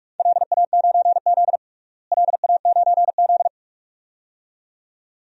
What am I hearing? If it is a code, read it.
Morse code, heard as a keyed tone, about 46 wpm, 700 Hz. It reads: LA9Z LA9Z
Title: DXCC-100 Call Signs - 4 Characters - Repeated 2x - 45wpm